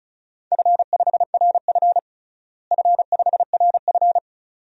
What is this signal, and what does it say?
Morse code, heard as a keyed tone, about 36 wpm, 700 Hz. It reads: F5RF F5RF
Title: DXCC-100 Call Signs - 4 Characters - Repeated 2x - 35wpm